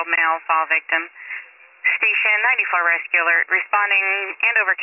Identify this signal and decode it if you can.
Emergency Alert System (SAME) tones, heard as harsh buzzing bursts, with speech. It is heard from 1.0 to 3.6 s.